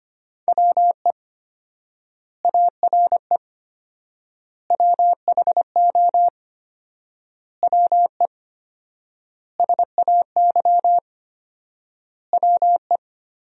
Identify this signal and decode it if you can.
Morse code, heard as a keyed tone, about 25 wpm, 700 Hz. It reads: WE ARE WHO WE SAY WE
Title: 4x Word-Spacing - Sentences from Top 200 Words - 25wpm